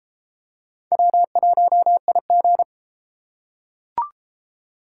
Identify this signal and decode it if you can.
Morse code, heard as a keyed tone, about 32 wpm, 700 Hz. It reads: W1IG